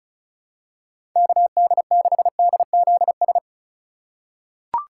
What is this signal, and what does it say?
Morse code, heard as a keyed tone, about 34 wpm, 700 Hz. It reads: KD6DZS
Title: USA Call Signs - 6 Characters - 2x Character Spacing - 35wpm